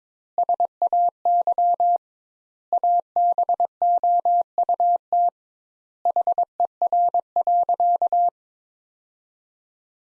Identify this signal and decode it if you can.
Morse code, heard as a keyed tone, about 22 wpm, 700 Hz. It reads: SAY ABOUT HER.
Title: Sentences Top 200 Words - 22wpm